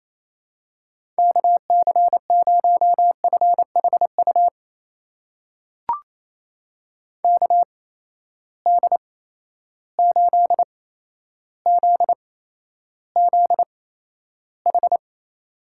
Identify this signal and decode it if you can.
Morse code, heard as a keyed tone, about 28 wpm, spaced as for 12 wpm, 700 Hz. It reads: KC0FHU KD8ZZH
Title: USA Call Signs - 6 Characters - 8x Character Spacing - 28wpm